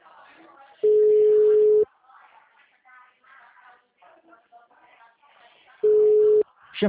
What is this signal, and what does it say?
Ringback tone (Europe, 425 Hz), with speech.
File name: out- -2012- - - .51454